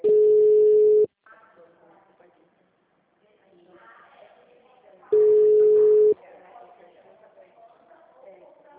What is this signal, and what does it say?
Ringback tone (Europe, 425 Hz), with speech.
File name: out- -2007- - - .41196